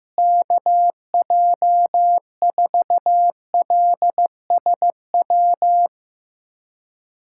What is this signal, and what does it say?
Morse code, heard as a keyed tone, about 15 wpm, 700 Hz. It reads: KJ4LSW
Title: USA Callsign DE Callsign 15wpm